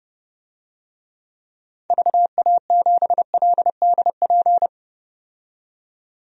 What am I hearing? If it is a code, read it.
Morse code, heard as a keyed tone, about 30 wpm, 700 Hz. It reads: VA7LDP
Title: DXCC-100 Call Signs - 6 Characters - 30wpm